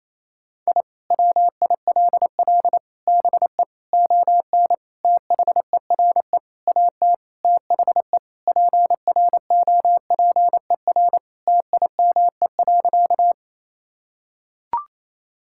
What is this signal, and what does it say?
Morse code, heard as a keyed tone, about 28 wpm, 700 Hz. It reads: I WILL BE ON THERE AT THE PROPER TIME.
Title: Sentences from Top 1000 Words - Review Part 5 - 28wpm